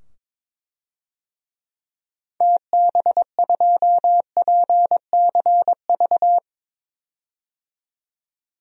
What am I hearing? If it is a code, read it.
Morse code, heard as a keyed tone, about 22 wpm, 700 Hz. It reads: TB2PCV
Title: DXCC-100 Call Signs - 6 Characters - 22wpm